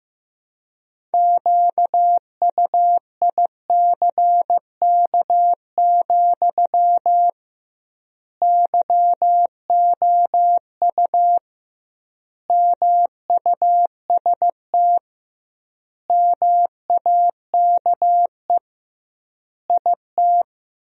Morse code, heard as a keyed tone, about 16 wpm, 700 Hz. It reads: QUICK, YOU MUST MAKE IT
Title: 2x Word-Spacing - Sentences from Top 500 Words - 15wpm